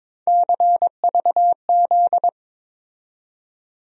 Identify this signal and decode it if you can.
Morse code, heard as a keyed tone, about 22 wpm, 700 Hz. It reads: CVZ
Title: USA Call Sign Postfixes - 3 Letters 22wpm